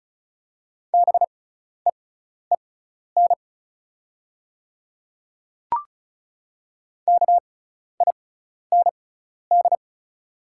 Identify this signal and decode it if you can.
Morse code, heard as a keyed tone, about 34 wpm, spaced as for 12 wpm, 700 Hz. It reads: BEEN KIND
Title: 6x Character Spacing - Top 200 Words - 35wpm